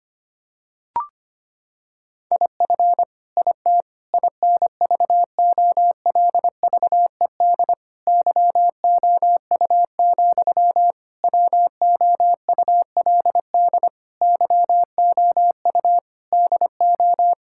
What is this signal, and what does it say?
Morse code, heard as a keyed tone, about 25 wpm, 700 Hz. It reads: IF IT INVOLVED YOU, WOULD YOU DO
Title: Sentences from Top 1000 Words - Review Part 5 - 25wpm